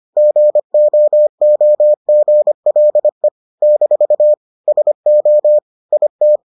Morse code, heard as a keyed tone, about 25 wpm, 600 Hz. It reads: GOOGLE - SO IT